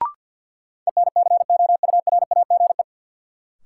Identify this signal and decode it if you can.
Morse code, heard as a keyed tone, about 52 wpm, 700 Hz. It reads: ENCOURAGE
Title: Sentences from Top 1400 Words - New Word Drill - 50wpm